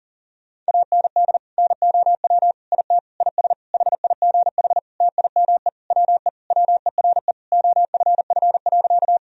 Morse code, heard as a keyed tone, about 40 wpm, 700 Hz. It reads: AND NOW IT IS HIGH TIME WE WERE OFF.
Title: Sentences from Top 1000 Words - Review Part 1 - 40wpm